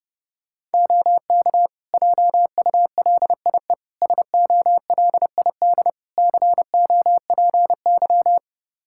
Morse code, heard as a keyed tone, about 30 wpm, 700 Hz. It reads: OK JULIE SOLID COPY
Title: Practice QSO - Exchange 3 - More Info - 30wpm